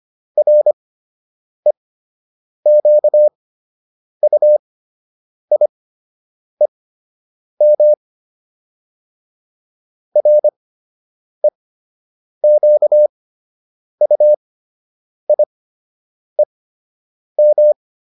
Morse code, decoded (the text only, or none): REQUIEM REQUIEM